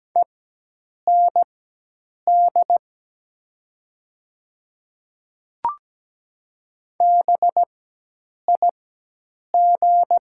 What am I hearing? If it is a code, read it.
Morse code, heard as a keyed tone, about 17 wpm, spaced as for 8 wpm, 700 Hz. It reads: END BIG
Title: 4x Character Spacing - Top 300 Words - 17wpm